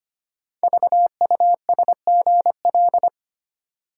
Morse code, heard as a keyed tone, about 25 wpm, 690 Hz. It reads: VUSGL
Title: Code Groups - 5 Letters - 25wpm